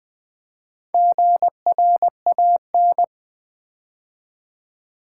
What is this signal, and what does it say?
Morse code, heard as a keyed tone, about 20 wpm, 700 Hz. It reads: GRAN
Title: Build A Word - 4 Letters At A Time - 11-Letter Words - 20wpm